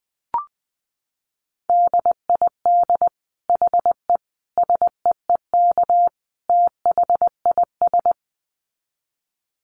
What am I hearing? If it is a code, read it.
Morse code, heard as a keyed tone, about 20 wpm, 700 Hz. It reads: DID HE SEEK THIS
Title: Dit Challenge 20wpm